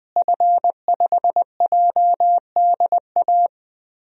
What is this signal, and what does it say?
Morse code, heard as a keyed tone, about 20 wpm, 700 Hz. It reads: F5JDA
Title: DXCC-100 Call Signs - 5 Characters - Repeated 3x - 20wpm